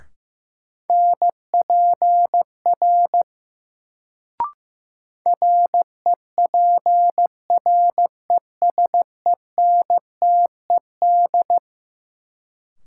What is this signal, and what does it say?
Morse code, heard as a keyed tone, about 16 wpm, 700 Hz. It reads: NPR REPRESENTED
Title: Sets of 5 Words - New Word Drill - 15wpm